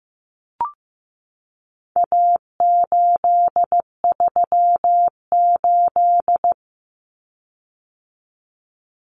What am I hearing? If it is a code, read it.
Morse code, heard as a keyed tone, about 15 wpm, 700 Hz. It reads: A838
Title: Common Cut Numbers - 4-Digit 15wpm